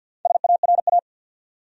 Morse code, heard as a keyed tone, about 51 wpm, 700 Hz. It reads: SARA
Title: USA Common Names - Max 4 Letters - 50wpm